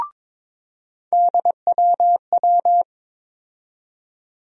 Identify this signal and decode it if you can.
Morse code, heard as a keyed tone, about 22 wpm, 700 Hz. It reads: DWW
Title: USA Call Sign Postfixes - 3 Letters 22wpm